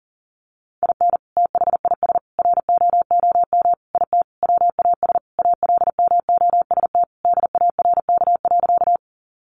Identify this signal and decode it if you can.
Morse code, heard as a keyed tone, about 40 wpm, 700 Hz. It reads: IN THIS ROOM IT WAS ALMOST DARK.
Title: Sentences from Top 700 Words 40wpm